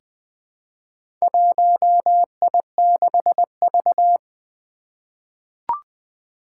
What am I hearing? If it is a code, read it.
Morse code, heard as a keyed tone, about 20 wpm, 700 Hz. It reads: 1I6V
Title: Four Challenging Letter-Number Combinations 20wpm